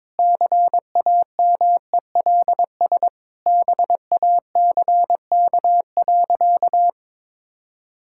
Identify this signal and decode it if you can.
Morse code, heard as a keyed tone, about 22 wpm, 700 Hz. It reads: CAMELS BACK.
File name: practice / 22wpm / Sentences with 8-Word Phrases - 22wpm